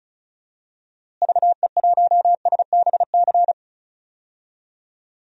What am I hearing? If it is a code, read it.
Morse code, heard as a keyed tone, about 36 wpm, 700 Hz. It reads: VE1SBC
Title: DXCC-100 Call Signs - 6 Characters - Repeated 2x - 35wpm